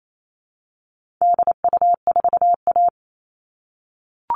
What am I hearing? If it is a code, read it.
Morse code, heard as a keyed tone, about 28 wpm, 700 Hz. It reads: DU4A
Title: All Cut Numbers - 4-Digit 28wpm